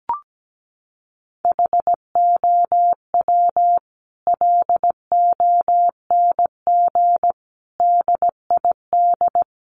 Morse code, heard as a keyed tone, about 17 wpm, 700 Hz. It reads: HOW LONG DID
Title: Sentences from Top 1500 Words 17wpm